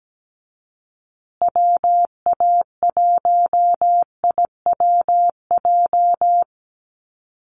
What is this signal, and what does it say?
Morse code, heard as a keyed tone, about 17 wpm, 700 Hz. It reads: WA1IWJ
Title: USA Call Signs Repeated 2x 17wpm